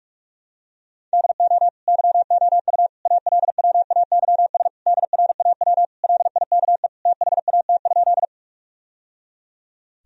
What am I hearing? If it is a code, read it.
Morse code, heard as a keyed tone, about 42 wpm, 700 Hz. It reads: DO YOU ALWAYS DRAW LIKE THAT?
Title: Sentences from Top 1000 Words - Review Part 1 - 45wpm